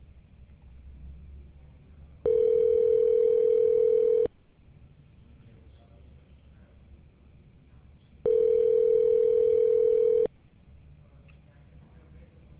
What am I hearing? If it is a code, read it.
Ringback tone (North America), 440 + 480 Hz.